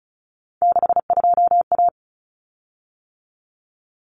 Morse code, heard as a keyed tone, about 34 wpm, 700 Hz. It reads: 62A
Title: Common Cut Numbers - 3-Digit 35wpm